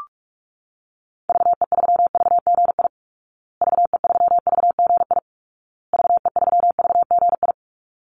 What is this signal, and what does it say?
Morse code, heard as a keyed tone, about 44 wpm, 700 Hz. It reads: VE3VGI VE3VGI VE3VGI
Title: Call Signs Repeated 2x - 45wpm